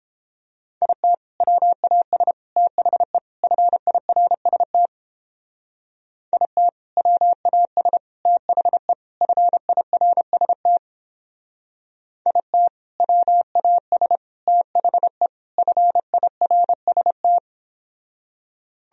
Morse code, heard as a keyed tone, about 29 wpm, 700 Hz. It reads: IT WAS THE FIRST IT WAS THE FIRST IT WAS THE FIRST